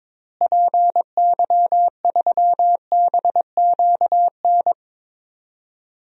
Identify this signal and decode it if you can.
Morse code, heard as a keyed tone, about 22 wpm, 700 Hz. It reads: PY3BQN
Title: DXCC-100 Call Signs - 6 Characters - Repeated 2x - 22wpm